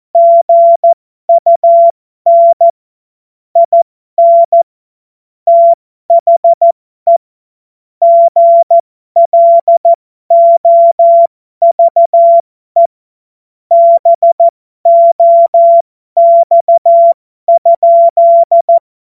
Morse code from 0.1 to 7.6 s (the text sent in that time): GUN IN THE